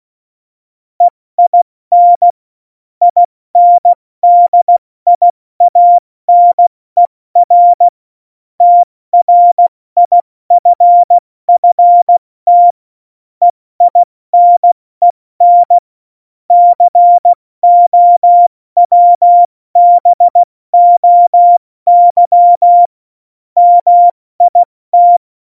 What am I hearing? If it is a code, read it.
Morse code, heard as a keyed tone, about 16 wpm, 700 Hz. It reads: EIN INDIANER TRIFFT EINEN COWBOY MIT
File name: Morsekurs / Witze / Witz 06. Oktober 2020, 70 F80